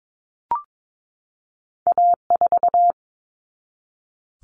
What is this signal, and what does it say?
Morse code, heard as a keyed tone, about 22 wpm, 700 Hz. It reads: A4
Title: Common Cut Numbers - 2-Digit 22wpm